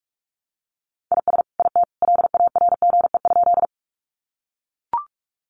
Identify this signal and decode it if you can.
Morse code, heard as a keyed tone, about 44 wpm, 700 Hz. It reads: IS IT LARGE?
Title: Sentences from Top 200 Words 45wpm